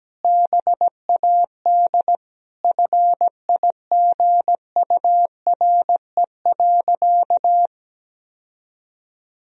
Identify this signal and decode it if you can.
Morse code, heard as a keyed tone, about 17 wpm, 700 Hz. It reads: BAD FIGURE.
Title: Sentences from Top 1000 Words - Review Part 2 - 17wpm